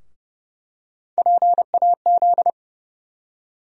Morse code, heard as a keyed tone, about 30 wpm, 700 Hz. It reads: PAZ
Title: USA Call Sign Postfixes - 3 Letters 30wpm